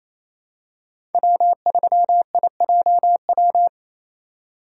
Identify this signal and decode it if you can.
Morse code, heard as a keyed tone, about 28 wpm, 700 Hz. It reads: W3IJW
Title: USA Call Signs - 8x Character Spacing - 28wpm